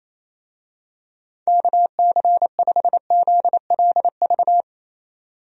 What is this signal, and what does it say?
Morse code, heard as a keyed tone, about 28 wpm, 700 Hz. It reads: KC5ZLV
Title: USA Call Signs - 6 Characters - 28wpm